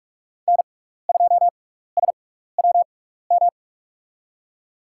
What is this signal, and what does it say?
Morse code, heard as a keyed tone, about 45 wpm, spaced as for 15 wpm, 700 Hz. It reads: N2SWM
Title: USA Call Signs - 5 Characters - 6x Character Spacing - 45wpm